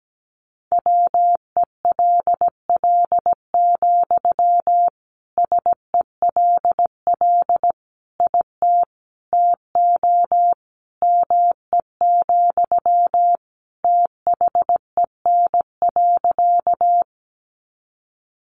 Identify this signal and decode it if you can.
Morse code, heard as a keyed tone, about 17 wpm, 700 Hz. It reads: WELL, SELL IT TO ME, THEN.
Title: Sentences from Top 1100 Words 17wpm